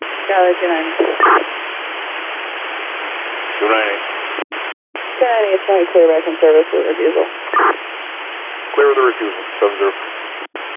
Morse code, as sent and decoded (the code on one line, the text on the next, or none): none